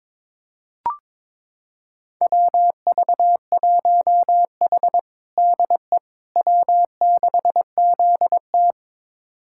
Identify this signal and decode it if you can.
Morse code, heard as a keyed tone, about 22 wpm, 700 Hz. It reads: WV1H DE W6ZT
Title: USA Callsign DE Callsign - 4 Characters - 22wpm